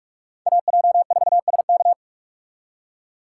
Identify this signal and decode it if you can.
Morse code, heard as a keyed tone, about 45 wpm, 700 Hz. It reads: AJ4SK